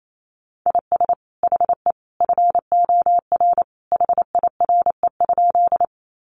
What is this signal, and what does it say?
Morse code, heard as a keyed tone, about 28 wpm, 700 Hz. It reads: IS HE FOR HIRE?
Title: Sentences from Top 1400 Words 28wpm